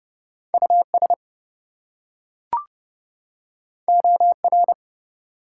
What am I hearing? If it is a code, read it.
Morse code, heard as a keyed tone, about 29 wpm, 700 Hz. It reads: US OR
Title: Two Letter Words - 30wpm